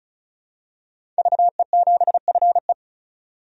Morse code, heard as a keyed tone, about 34 wpm, 700 Hz. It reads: VE7FE
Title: DXCC-100 Call Signs - 5 Characters - 35wpm